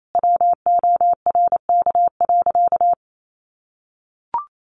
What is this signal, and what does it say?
Morse code, heard as a keyed tone, about 28 wpm, 700 Hz. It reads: WORK.
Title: Sentences from Top 1300 Words 28wpm